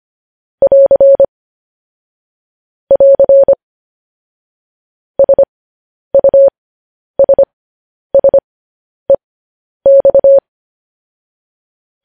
Morse code, heard as a keyed tone, about 25 wpm, spaced as for 10 wpm, 550 Hz. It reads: <AR> <AR> SUSSEX
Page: news-headlines-in-morse-code-at-10-wpm